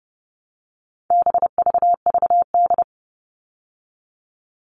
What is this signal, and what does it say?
Morse code, heard as a keyed tone, about 30 wpm, 700 Hz. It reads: BVVD
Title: All Cut Numbers - 4-Digit 30wpm